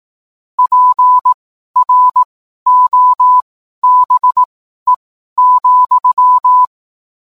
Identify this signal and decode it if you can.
Morse code, heard as a keyed tone, about 18 wpm, spaced as for 12 wpm, 1000 Hz. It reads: PROBE,